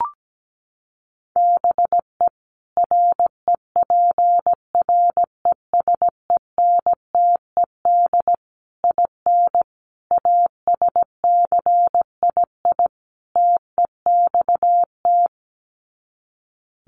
Morse code, heard as a keyed tone, about 17 wpm, 700 Hz. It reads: BE REPRESENTED IN ASCII TEXT
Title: Sets of 5 Words 17wpm